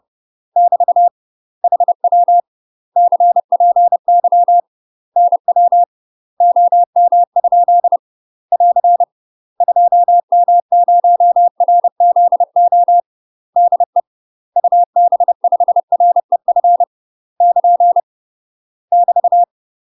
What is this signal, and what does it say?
Morse code, heard as a keyed tone, about 30 wpm, 700 Hz. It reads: <BT> HW CPY NW OM? <AR> 2M0RZO DE UB5REF <KN> <BT>
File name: UK MASTER QSOs TRAINING / 30WPM / UK MASTER QSOs 0044